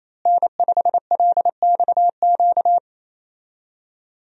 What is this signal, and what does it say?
Morse code, heard as a keyed tone, about 28 wpm, 700 Hz. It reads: N5LXQ
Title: USA Call Signs - 5 Characters - Repeated 2x - 28wpm